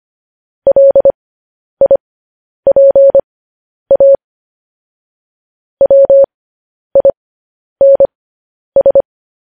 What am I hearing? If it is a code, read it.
Morse code, heard as a keyed tone, about 25 wpm, spaced as for 10 wpm, 550 Hz. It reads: LIPA WINS
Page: news-headlines-in-morse-code-at-10-wpm